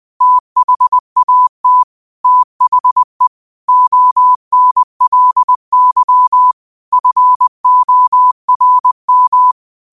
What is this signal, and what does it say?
Morse code, heard as a keyed tone, about 20 wpm, 1000 Hz. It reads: THAT THE ONLY FORM
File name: qotdfiles / 20wpm